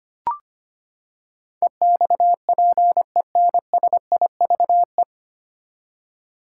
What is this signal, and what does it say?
Morse code, heard as a keyed tone, about 25 wpm, 700 Hz. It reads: EXPENSIVE
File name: Nine Letter Words - 25wpm